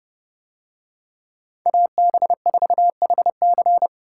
Morse code, heard as a keyed tone, about 29 wpm, 700 Hz. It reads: AB4HC